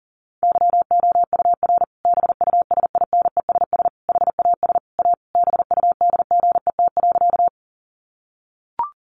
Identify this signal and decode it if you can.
Morse code, heard as a keyed tone, about 40 wpm, 700 Hz. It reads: YOUR BUSINESS HAS A BUDGET.
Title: Sentences from Top 1200 Words 40wpm